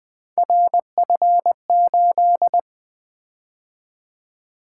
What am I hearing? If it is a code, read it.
Morse code, heard as a keyed tone, about 20 wpm, 700 Hz. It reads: RF8
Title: Three Letter-Number Combinations - 20wpm